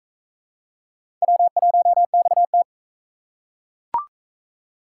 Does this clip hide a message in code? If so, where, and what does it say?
Yes: Morse code from 1.2 to 2.6 s, heard as a keyed tone, about 40 wpm, 700 Hz. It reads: W1XT